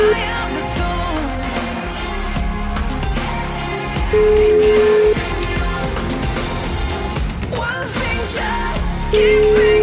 Ringback tone (Europe, 425 Hz), with noise.